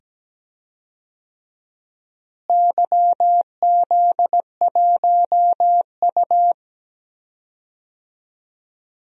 Morse code, heard as a keyed tone, about 17 wpm, 700 Hz. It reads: YZ1U
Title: DXCC-100 Call Signs - 4 Characters - Repeated 2x - 17wpm